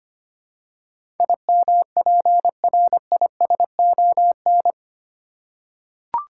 Morse code, heard as a keyed tone, about 25 wpm, 700 Hz. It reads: IMPRISON
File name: Eight Letter Words - 25wpm